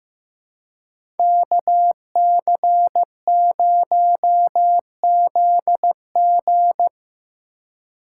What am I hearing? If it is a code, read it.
Morse code, heard as a keyed tone, about 18 wpm, 700 Hz. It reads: KC0ZG
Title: Speed Racing - USA 2x2 Call Signs 15wpm